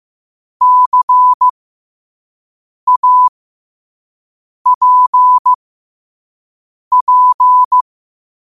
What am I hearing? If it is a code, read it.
Morse code, heard as a keyed tone, about 15 wpm, spaced as for 5 wpm, 1000 Hz. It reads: CAPP
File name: qotdfiles / 5wpm